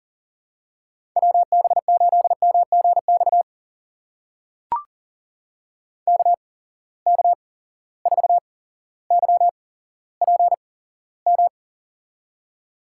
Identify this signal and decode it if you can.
Morse code, heard as a keyed tone, about 38 wpm, spaced as for 17 wpm, 700 Hz. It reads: WB8MGX KK4YPM
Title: USA Call Signs - 6 Characters - 8x Character Spacing - 40wpm